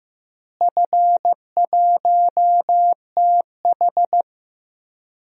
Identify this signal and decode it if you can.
Morse code, heard as a keyed tone, about 15 wpm, 700 Hz. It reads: F1TH
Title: DXCC-100 Call Signs - 4 Characters - Repeated 3x - 15wpm